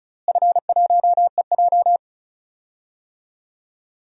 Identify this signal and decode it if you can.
Morse code, heard as a keyed tone, about 34 wpm, 700 Hz. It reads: F1EJ
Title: DXCC-100 Call Signs - 4 Characters - Repeated 2x - 35wpm